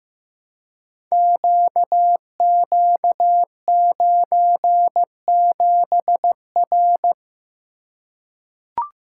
Morse code, heard as a keyed tone, about 15 wpm, 700 Hz. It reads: QQ97R
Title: Code Groups - 5 Letter:Numbers - 15wpm